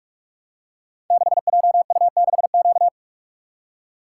Morse code, heard as a keyed tone, about 44 wpm, 700 Hz. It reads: 6JU6Q
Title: Code Groups - 5 Letter:Numbers - 45wpm